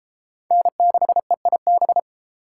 Morse code, heard as a keyed tone, about 33 wpm, 700 Hz. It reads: N6EIB